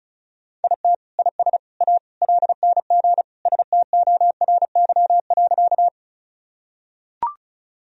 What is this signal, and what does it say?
Morse code, heard as a keyed tone, about 36 wpm, 700 Hz. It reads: IT IS A LNG STORY.
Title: Sentences from Top 300 Words and Abbreviations 35wpm